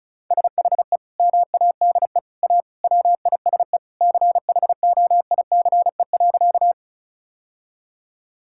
Morse code, heard as a keyed tone, about 36 wpm, 700 Hz. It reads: SHE MADE A WISE CHOICE.
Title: Sentences from Top 1600 Words 35wpm